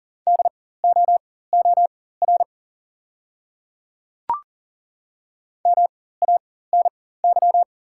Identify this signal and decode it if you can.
Morse code, heard as a keyed tone, about 38 wpm, spaced as for 18 wpm, 700 Hz. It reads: DOOR MANY
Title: 4x Character Spacing - Top 300 Words - 40wpm